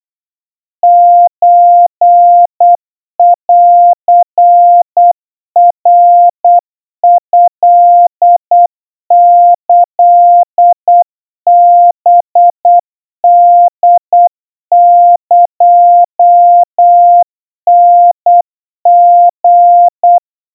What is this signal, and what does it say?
Morse code (Japanese Wabun code), heard as a keyed tone, about 8 wpm, 700 Hz. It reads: ソンナトキハホエタリ
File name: kawa / Morse sample / kamyu 30 cpm 700 Hz Crispy